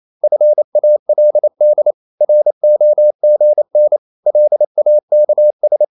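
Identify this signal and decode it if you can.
Morse code, heard as a keyed tone, about 28 wpm, 600 Hz. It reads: FALD ROGN LAKS
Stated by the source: Danish short words sent in Morse at 28 wpm, from CW med Gnister.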